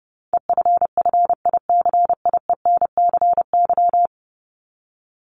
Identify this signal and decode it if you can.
Morse code, heard as a keyed tone, about 30 wpm, 700 Hz. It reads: EFFICIENCY
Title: Top 1501-1750 Words 30wpm